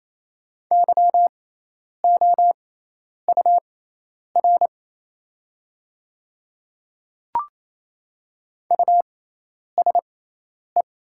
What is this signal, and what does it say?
Morse code, heard as a keyed tone, about 27 wpm, spaced as for 10 wpm, 690 Hz. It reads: YOUR USE
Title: 6x Character Spacing - Top 100 Words - 28wpm